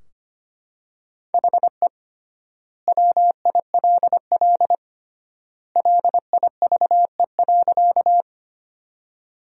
Morse code, heard as a keyed tone, about 25 wpm, 700 Hz. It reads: HE WILL LIVE.
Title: 3x Word-Spacing - Sentences from Top 200 Words - 25wpm